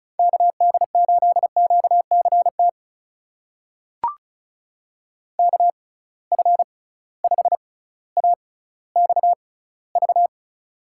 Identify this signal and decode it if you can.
Morse code, heard as a keyed tone, about 36 wpm, spaced as for 18 wpm, 700 Hz. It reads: KD8QCT KF5AXV
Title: USA Call Signs - 6 Characters - 6x Character Spacing - 35wpm